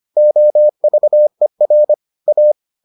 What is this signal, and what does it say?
Morse code, heard as a keyed tone, about 25 wpm, 600 Hz. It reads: OVER A